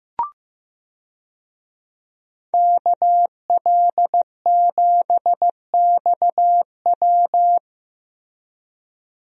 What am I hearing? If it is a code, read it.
Morse code, heard as a keyed tone, about 15 wpm, 700 Hz. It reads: KL7XW
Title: DXCC-100 Call Signs - 5 Characters - Repeated 2x - 15wpm